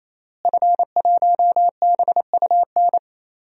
Morse code, heard as a keyed tone, about 28 wpm, 700 Hz. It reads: F1BUN
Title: DXCC-100 Call Signs - 5 Characters - Repeated 3x - 28wpm